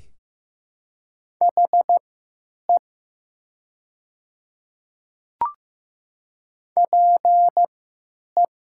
Morse code, heard as a keyed tone, about 15 wpm, spaced as for 9 wpm, 700 Hz. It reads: HE PE
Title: 3x Character Spacing - Top 100 Words - 15wpm